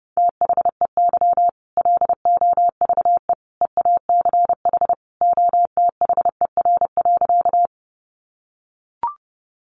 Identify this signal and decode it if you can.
Morse code, heard as a keyed tone, about 30 wpm, 700 Hz. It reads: THEY LOVE EACH OTHER.
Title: Easy Phrases 30wpm